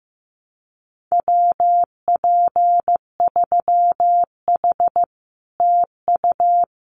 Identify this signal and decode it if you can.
Morse code, heard as a keyed tone, about 15 wpm, 700 Hz. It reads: WP3H TU
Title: Common QSO Phrases Encore 15wpm